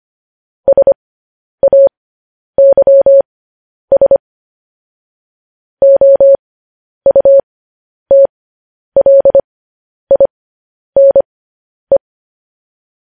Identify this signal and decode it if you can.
Morse code, heard as a keyed tone, about 25 wpm, spaced as for 10 wpm, 550 Hz. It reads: SAYS OUTLINE
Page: news headlines in morse code at 10 wpm